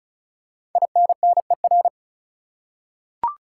Morse code, heard as a keyed tone, about 36 wpm, 700 Hz. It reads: INNER
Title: Five Letter Words - 35wpm